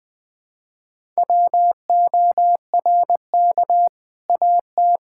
Morse code, heard as a keyed tone, about 20 wpm, 700 Hz. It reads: WORK AT
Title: Sentences from Top 1000 Words - Review Part 5 - 20wpm